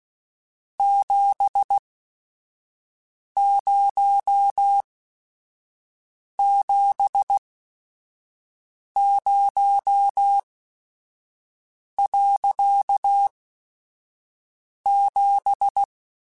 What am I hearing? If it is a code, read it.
Morse code, heard as a keyed tone, about 16 wpm, spaced as for 5 wpm, 800 Hz. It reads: 7070.7